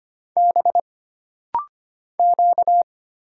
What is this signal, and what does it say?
Morse code, heard as a keyed tone, about 25 wpm, 700 Hz. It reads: B Q
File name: Single Character (Letter:Number:Punctuation:Prosign) - Rapid-Fire - 25wpm